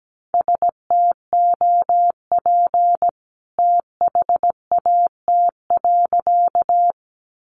Morse code, heard as a keyed tone, about 17 wpm, 700 Hz. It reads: STOP THAT.
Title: A Sound Of Thunder 17wpm